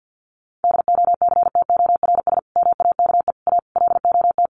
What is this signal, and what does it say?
Morse code, heard as a keyed tone, about 50 wpm, 700 Hz. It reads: DOCTORS MAKE A LOT
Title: Easy Phrases 50wpm